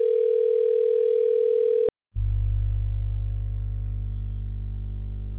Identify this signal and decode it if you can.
Ringback tone (North America), 440 + 480 Hz.